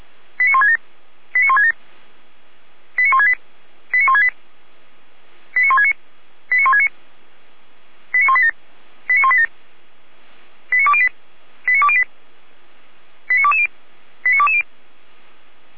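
Five-tone selective call (ZVEI), heard as quick runs of five tones, tones of about 70 ms. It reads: 89167 89167 89168 89168 89169 89169 89187 89187 89208 89208 89220 89220
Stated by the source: ZVEI five-tone selective call received on an SDR.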